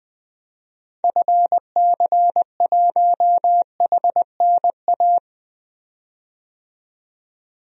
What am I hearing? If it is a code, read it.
Morse code, heard as a keyed tone, about 21 wpm, 700 Hz. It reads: FC1HNA
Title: DXCC-100 Call Signs - 6 Characters - 20wpm